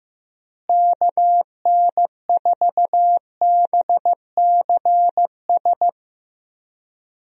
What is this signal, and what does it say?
Morse code, heard as a keyed tone, about 15 wpm, 700 Hz. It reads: KN4BCS
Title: USA Call Signs - 6 Characters - Repeated 2x - 15wpm